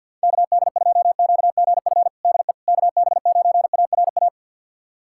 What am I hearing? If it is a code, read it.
Morse code, heard as a keyed tone, about 51 wpm, 700 Hz. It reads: KD2QGF DE KB9ARA
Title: USA Callsign DE Callsign - 6 Characters - 50wpm